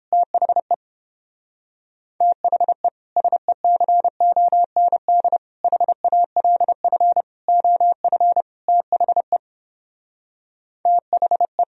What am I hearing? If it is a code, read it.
Morse code, heard as a keyed tone, about 31 wpm, 700 Hz. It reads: THE THE SECOND HALF OF THE THE